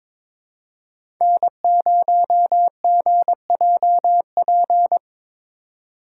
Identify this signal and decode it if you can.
Morse code, heard as a keyed tone, about 22 wpm, 700 Hz. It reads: N0GJP
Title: USA Call Signs - 5 Characters - 3x Character Spacing - 22wpm